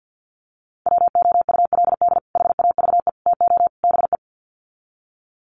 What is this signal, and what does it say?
Morse code, heard as a keyed tone, about 50 wpm, 700 Hz. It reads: WOULD HAVE TO BE
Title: Sets of 4 Words 50wpm